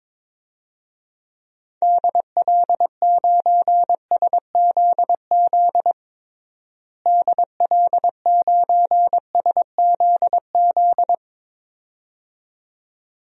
Morse code, heard as a keyed tone, about 22 wpm, 700 Hz. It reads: DL9SZZ DL9SZZ
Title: DXCC-100 Call Signs - 6 Characters - Repeated 2x - 22wpm